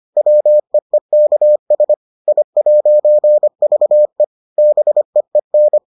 Morse code, heard as a keyed tone, about 25 wpm, 600 Hz. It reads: WEEKS I'VE BEEN